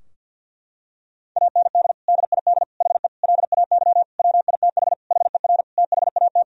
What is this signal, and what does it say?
Morse code, heard as a keyed tone, about 52 wpm, 700 Hz. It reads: AND DID HE LAY WITH HER THAT
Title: Sentences from Top 1000 Words - Review Part 2 - 50wpm